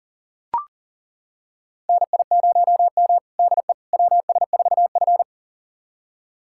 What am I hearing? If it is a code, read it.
Morse code, heard as a keyed tone, about 40 wpm, 700 Hz. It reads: NI0M DE WS4F
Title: USA Callsign DE Callsign - 4 Characters - 40wpm